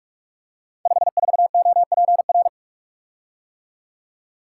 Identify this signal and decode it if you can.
Morse code, heard as a keyed tone, about 44 wpm, 700 Hz. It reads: 54OPR
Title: Code Groups - 5 Letter:Numbers - 45wpm